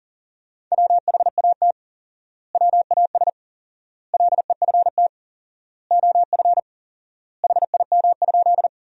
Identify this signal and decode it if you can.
Morse code, heard as a keyed tone, about 40 wpm, 700 Hz. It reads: WHAT WAS LEFT OF HIM?
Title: 4x Word-Spacing - Sentences from Top 300 Words - 40wpm